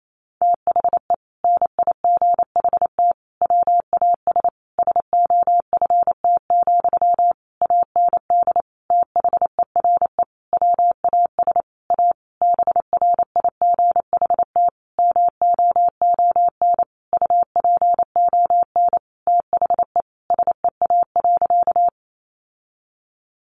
Morse code, heard as a keyed tone, about 28 wpm, 700 Hz. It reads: THE NIGHT WAS SOFT, AND THERE WAS A BRIGHT MOON UPON THE SEA.
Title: Sentences from Top 1500 Words 28wpm